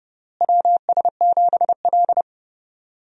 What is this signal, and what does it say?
Morse code, heard as a keyed tone, about 30 wpm, 700 Hz. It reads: WS7L
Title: USA Call Signs - 4 Characters - Repeated 2x - 30wpm